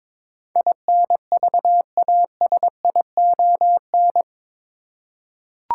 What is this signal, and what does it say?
Morse code, heard as a keyed tone, about 22 wpm, 700 Hz. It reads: INVASION
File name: practice / 22wpm / Eight Letter Words - 22wpm